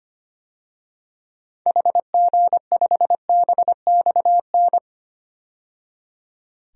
Morse code, heard as a keyed tone, about 25 wpm, 700 Hz. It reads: HG5BXN
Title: DXCC-100 Call Signs - 6 Characters - 25wpm